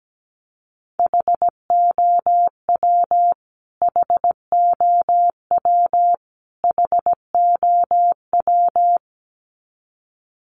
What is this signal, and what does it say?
Morse code, heard as a keyed tone, about 17 wpm, 700 Hz. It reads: HOW HOW HOW
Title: Top 200 Words Repeated 17wpm